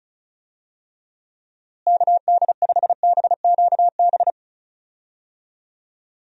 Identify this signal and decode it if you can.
Morse code, heard as a keyed tone, about 34 wpm, 700 Hz. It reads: KD5BQB
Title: USA Call Signs - 6 Characters - Repeated 2x - 35wpm